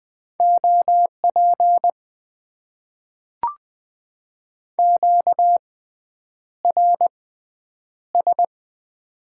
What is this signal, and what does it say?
Morse code, heard as a keyed tone, about 20 wpm, spaced as for 9 wpm, 700 Hz. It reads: OP QRS
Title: Top 90 QSO Elements - 6x Character Spacing - 20wpm